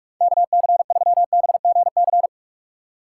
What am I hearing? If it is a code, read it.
Morse code, heard as a keyed tone, about 58 wpm, 700 Hz. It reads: KC3BGC